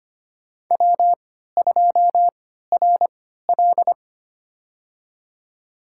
Morse code, heard as a keyed tone, about 25 wpm, spaced as for 14 wpm, 700 Hz. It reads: W2RL
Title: USA Call Signs - 3x Character Spacing - 25wpm